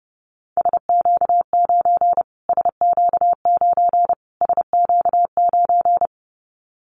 Morse code, heard as a keyed tone, about 30 wpm, 700 Hz. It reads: SQ9 SQ9 SQ9
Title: Callsign Prefixes for Top 25 DXCC - Repeated 3x - 30wpm